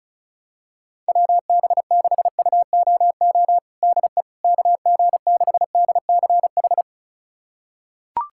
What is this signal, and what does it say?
Morse code, heard as a keyed tone, about 36 wpm, 700 Hz. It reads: WB6UOO DE KG6DCH
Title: USA Callsign DE Callsign 35wpm